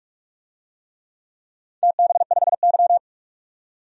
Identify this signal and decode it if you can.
Morse code, heard as a keyed tone, about 46 wpm, 700 Hz. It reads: TB5Y
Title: DXCC-100 Call Signs - 4 Characters - 45wpm